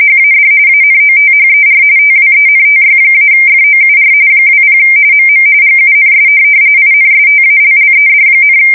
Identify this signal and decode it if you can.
RTTY, about 45.45 baud. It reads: HAD ANOTHER TWENTY-FOUR 20 MM GUNS INSTALLED,
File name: fsk170 / USS-Massachusetts